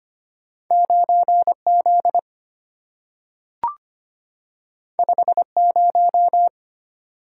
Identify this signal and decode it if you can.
Morse code, heard as a keyed tone, about 25 wpm, 700 Hz. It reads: 9Z 50
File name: Two Letter-Number Combinations - 25wpm